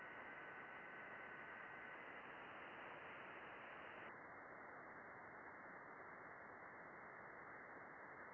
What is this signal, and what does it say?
FT8.
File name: contests / june-vhf-k5tr- -0200